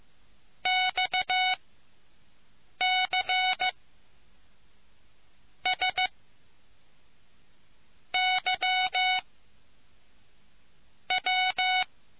Morse code, heard as a keyed tone, about 15 wpm, 740 Hz. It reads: X C S Y W